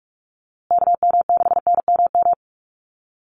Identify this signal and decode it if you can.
Morse code, heard as a keyed tone, about 44 wpm, 700 Hz. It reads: KM6NMM
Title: USA Call Signs Repeated 2x 45wpm